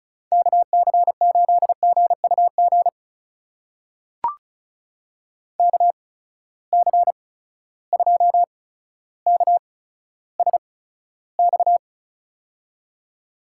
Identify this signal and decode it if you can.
Morse code, heard as a keyed tone, about 34 wpm, spaced as for 15 wpm, 700 Hz. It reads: KC8GUG KC2KSX
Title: USA Call Signs - 6 Characters - 8x Character Spacing - 35wpm